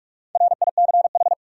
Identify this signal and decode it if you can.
Morse code, heard as a keyed tone, about 41 wpm, 700 Hz. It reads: RICH
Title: USA Common Names - Max 5 Letters - 45wpm